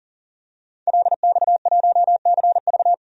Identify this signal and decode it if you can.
Morse code, heard as a keyed tone, about 39 wpm, 700 Hz. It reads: LX1CV
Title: DXCC-100 Call Signs - 5 Characters - 40wpm